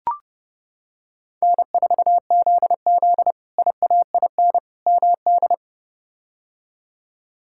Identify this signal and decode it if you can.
Morse code, heard as a keyed tone, about 30 wpm, 700 Hz. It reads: N4ZZ IAIN MD
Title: K1USN SST QSO Practice - 30wpm